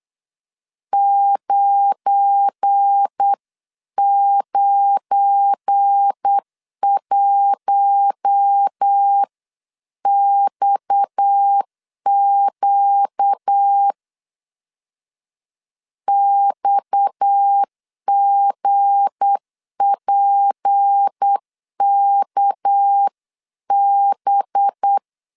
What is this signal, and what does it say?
Morse code, heard as a keyed tone, about 8 wpm, 800 Hz. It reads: 991XQ XGPKB
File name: LEZIONI / 7 GF9Z / 7 GF9 8wpm 3545